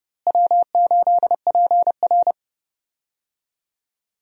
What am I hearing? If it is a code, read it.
Morse code, heard as a keyed tone, about 32 wpm, 700 Hz. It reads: W8PR
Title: USA Call Signs - 4 Characters - Repeated 2x - 30wpm